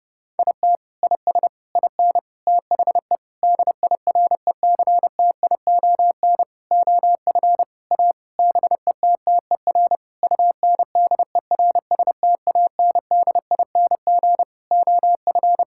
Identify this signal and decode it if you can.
Morse code, heard as a keyed tone, about 30 wpm, 700 Hz. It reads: IT IS IN THE DIRECTION OF A BETTER UNDERSTANDING OF
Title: Sentences from Top 1000 Words - Review Part 5 - 30wpm